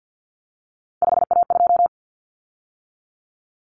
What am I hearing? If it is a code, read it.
Morse code, heard as a keyed tone, about 50 wpm, 700 Hz. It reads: 5A2
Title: Common Cut Numbers - 3-Digit 50wpm